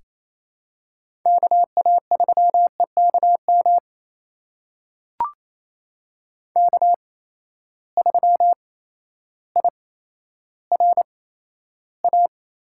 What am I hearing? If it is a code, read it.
Morse code, heard as a keyed tone, about 27 wpm, spaced as for 13 wpm, 700 Hz. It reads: KA3EKM K3IRA
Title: USA Call Signs - 8x Character Spacing - 28wpm